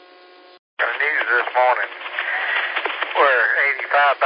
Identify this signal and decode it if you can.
Emergency Alert System (SAME) tones, heard as harsh buzzing bursts, with speech. It is heard from 0.8 to 3.8 s.